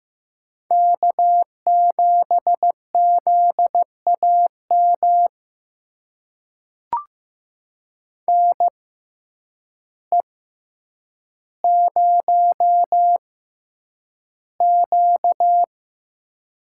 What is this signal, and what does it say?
Morse code, heard as a keyed tone, about 15 wpm, spaced as for 8 wpm, 700 Hz. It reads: K7ZAM NE0Q
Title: USA Call Signs - 5 Characters - 6x Character Spacing - 15wpm